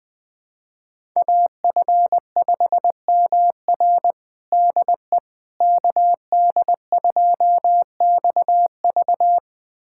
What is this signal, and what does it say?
Morse code, heard as a keyed tone, about 20 wpm, 700 Hz. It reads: AF5MR DE KD2XV
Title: USA Callsign DE Callsign - 5 Characters - 20wpm